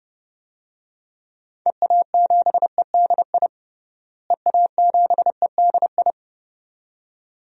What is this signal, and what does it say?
Morse code, heard as a keyed tone, about 31 wpm, 700 Hz. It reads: EA7EDI EA7EDI
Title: DXCC-100 Call Signs - 6 Characters - Repeated 2x - 30wpm